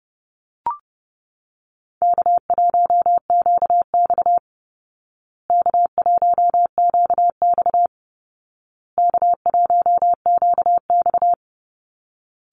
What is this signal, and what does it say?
Morse code, heard as a keyed tone, about 30 wpm, 700 Hz. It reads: K1QX K1QX K1QX
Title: Easy Call Signs Repeated 3x - 30wpm